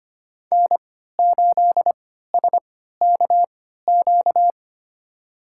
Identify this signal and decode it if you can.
Morse code, heard as a keyed tone, about 25 wpm, spaced as for 14 wpm, 700 Hz. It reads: N8SKQ
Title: USA Call Signs - 3x Character Spacing - 25wpm